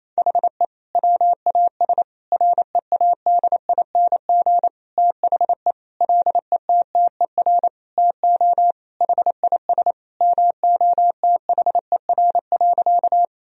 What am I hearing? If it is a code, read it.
Morse code, heard as a keyed tone, about 27 wpm, 700 Hz. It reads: HE WAS READING THE LETTER TO HIS MOTHER.
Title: Sentences from Top 1000 Words - Review Part 3 - 28wpm